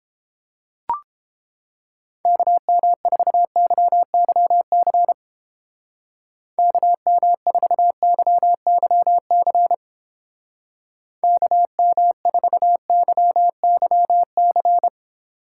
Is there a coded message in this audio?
Yes: Morse code from 2.2 to 14.9 s, heard as a keyed tone, about 31 wpm, 700 Hz. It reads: KM4YYC KM4YYC KM4YYC